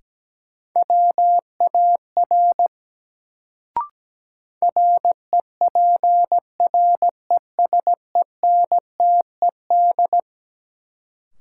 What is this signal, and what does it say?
Morse code, heard as a keyed tone, about 17 wpm, 700 Hz. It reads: WAR REPRESENTED
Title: Sets of 5 Words - New Word Drill - 17wpm